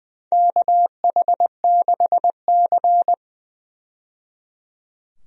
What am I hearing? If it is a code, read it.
Morse code, heard as a keyed tone, about 20 wpm, 700 Hz. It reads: KH6C
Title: DXCC-100 Call Signs - 4 Characters - 20wpm